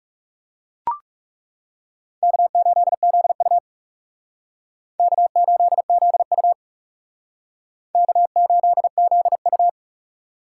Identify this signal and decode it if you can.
Morse code, heard as a keyed tone, about 40 wpm, 700 Hz. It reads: K8ZU K8ZU K8ZU